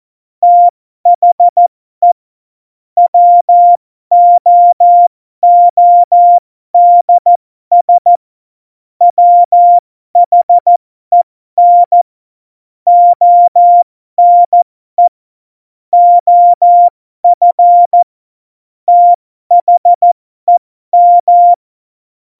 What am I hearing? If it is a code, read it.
Morse code, heard as a keyed tone, about 14 wpm, spaced as for 12 wpm, 700 Hz. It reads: THE WOODS WHEN ONE OF THEM
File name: Morsekurs / Witze